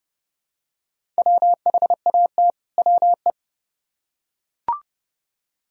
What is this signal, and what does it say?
Morse code, heard as a keyed tone, about 28 wpm, 700 Hz. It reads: WHAT WE